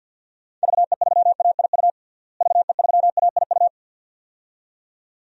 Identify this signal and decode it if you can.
Morse code, heard as a keyed tone, about 53 wpm, 700 Hz. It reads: VE3AIU VE3AIU
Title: DXCC-100 Call Signs - 6 Characters - Repeated 2x - 50wpm